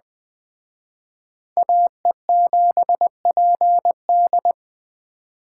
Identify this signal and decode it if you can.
Morse code, heard as a keyed tone, about 21 wpm, 700 Hz. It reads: AE7PD